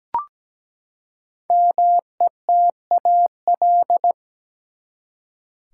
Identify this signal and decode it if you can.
Morse code, heard as a keyed tone, about 17 wpm, 700 Hz. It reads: METAL
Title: Five Letter Words - 17wpm